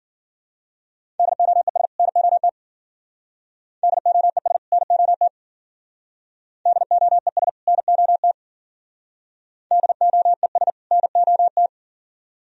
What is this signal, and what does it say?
Morse code, heard as a keyed tone, about 53 wpm, 700 Hz. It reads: DOES NOT DOES NOT DOES NOT DOES NOT